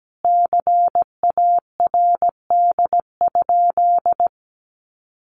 Morse code, heard as a keyed tone, about 17 wpm, 700 Hz. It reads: CARD?
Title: Sentences from Top 1100 Words 17wpm